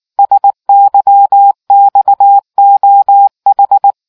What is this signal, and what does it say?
Morse code, heard as a keyed tone, about 19 wpm, 800 Hz. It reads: SYXOH